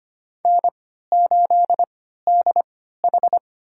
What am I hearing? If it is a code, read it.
Morse code, heard as a keyed tone, about 25 wpm, spaced as for 14 wpm, 700 Hz. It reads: N8DH
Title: USA Call Signs - 4 Characters - 3x Character Spacing - 25wpm